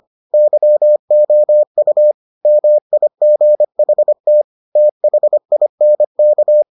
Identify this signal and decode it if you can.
Morse code, heard as a keyed tone, about 25 wpm, 600 Hz. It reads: YOU MIGHT THINK